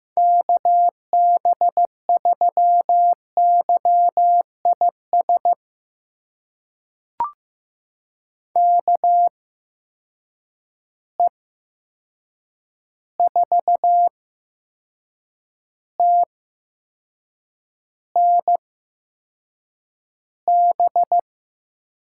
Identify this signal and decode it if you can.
Morse code, heard as a keyed tone, about 15 wpm, spaced as for 6 wpm, 700 Hz. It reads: KB3YIS KE4TNB
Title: USA Call Signs - 6 Characters - 8x Character Spacing - 15wpm